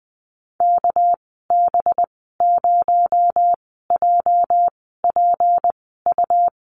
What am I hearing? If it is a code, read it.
Morse code, heard as a keyed tone, about 20 wpm, spaced as for 14 wpm, 700 Hz. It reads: KB0JPU
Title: 245 Lesson 39g - Letter z - Call Signs 20wpm